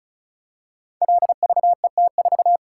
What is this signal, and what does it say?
Morse code, heard as a keyed tone, about 34 wpm, 700 Hz. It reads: LVET4
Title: Code Groups - 5 Letter:Numbers - 35wpm